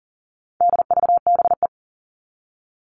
Morse code, heard as a keyed tone, about 40 wpm, 700 Hz. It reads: DVBE
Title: All Cut Numbers - 4-Digit 40wpm